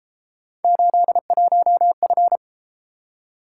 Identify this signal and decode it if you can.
Morse code, heard as a keyed tone, about 31 wpm, 700 Hz. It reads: 81F